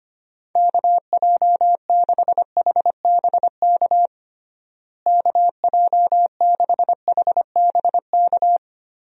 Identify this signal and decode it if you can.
Morse code, heard as a keyed tone, about 25 wpm, 700 Hz. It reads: KJ6HBK KJ6HBK
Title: USA Call Signs - 6 Characters - Repeated 3x - 25wpm